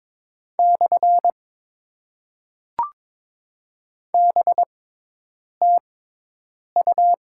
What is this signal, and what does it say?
Morse code, heard as a keyed tone, about 22 wpm, 700 Hz. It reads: / B T U
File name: Top 90 QSO Elements - 6x Character Spacing - 22wpm